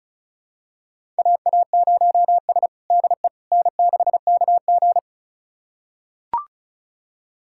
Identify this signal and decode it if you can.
Morse code, heard as a keyed tone, about 36 wpm, 700 Hz. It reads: AA0S DE N6KG
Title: USA Callsign DE Callsign - 4 Characters - 35wpm